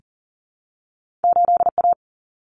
Morse code, heard as a keyed tone, about 40 wpm, 700 Hz. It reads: 8A
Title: Common Cut Numbers - 2-Digit 40wpm